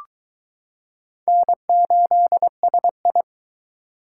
Morse code, heard as a keyed tone, about 22 wpm, 700 Hz. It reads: N8SI